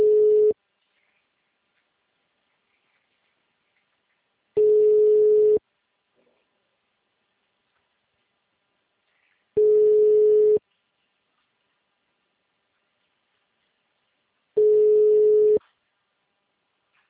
Ringback tone (Europe, 425 Hz).
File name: out- -2004- - - .36